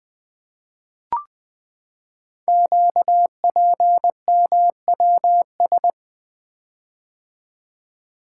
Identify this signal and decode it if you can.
Morse code, heard as a keyed tone, about 20 wpm, 700 Hz. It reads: QPMWS
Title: Code Groups - 5 Letters - 20wpm